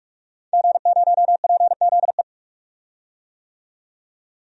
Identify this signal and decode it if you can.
Morse code, heard as a keyed tone, about 46 wpm, 700 Hz. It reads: G0PZE
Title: DXCC-100 Call Signs - 5 Characters - 45wpm